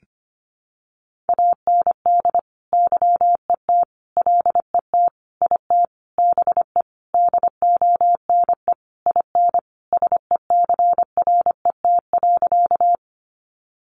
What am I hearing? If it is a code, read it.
Morse code, heard as a keyed tone, about 25 wpm, 700 Hz. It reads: AND YET LET IT BE DONE IN SECRET.
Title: Sentences from Top 1200 Words 25wpm